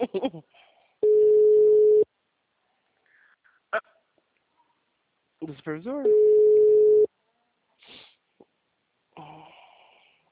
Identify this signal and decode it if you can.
Ringback tone (Europe, 425 Hz), with speech.
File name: out- -2001- - - .19